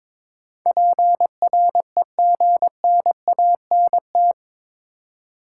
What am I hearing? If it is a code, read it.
Morse code, heard as a keyed tone, about 22 wpm, 700 Hz. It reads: PREGNANT
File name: practice / 22wpm / Eight Letter Words - 22wpm